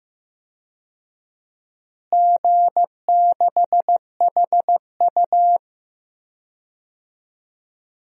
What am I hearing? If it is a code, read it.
Morse code, heard as a keyed tone, about 15 wpm, 700 Hz. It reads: G6HU
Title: DXCC-100 Call Signs - 4 Characters - Repeated 3x - 15wpm